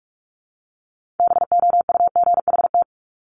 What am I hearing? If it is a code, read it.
Morse code, heard as a keyed tone, about 46 wpm, 700 Hz. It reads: BOUGHT
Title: Multi-letter Phonemes Repeated 3x 45wpm